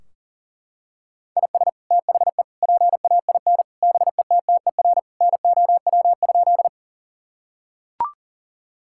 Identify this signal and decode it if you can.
Morse code, heard as a keyed tone, about 40 wpm, 700 Hz. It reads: IS THE PAIN BETTER NOW?
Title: Sentences from Top 1600 Words 40wpm